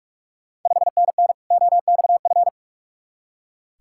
Morse code, heard as a keyed tone, about 42 wpm, 700 Hz. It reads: 5NN OXF
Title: Ontario QSO Party Exchange 45wpm